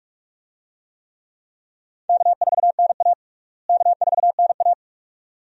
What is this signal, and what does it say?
Morse code, heard as a keyed tone, about 44 wpm, 700 Hz. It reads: K4NA K4NA
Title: USA Call Signs - 4 Characters - Repeated 2x - 45wpm